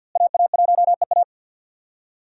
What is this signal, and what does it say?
Morse code, heard as a keyed tone, about 51 wpm, 700 Hz. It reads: AA1EA
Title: USA Call Signs - 5 Characters - Repeated 3x - 50wpm